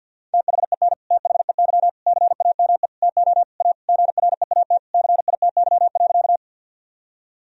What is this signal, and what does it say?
Morse code, heard as a keyed tone, about 50 wpm, 700 Hz. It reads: THEN THEY CAME TO A GREAT CITY.
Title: Sentences from Top 1000 Words - Review Part 1 - 50wpm